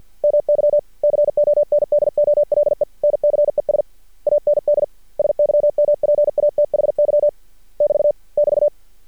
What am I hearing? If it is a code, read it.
Morse code, heard as a keyed tone, about 48 wpm, 590 Hz. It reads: MY CONDOLENCES AND SYMPATHY <BT> <BT>
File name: archived / CRSnet